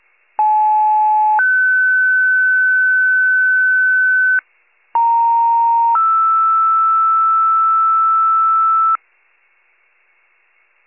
Two-tone page, heard as a one-second tone, then a three-second tone, about 860 then 1500 Hz, about 930 then 1320 Hz. It is heard from 0.4 to 9.0 s.
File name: wcalert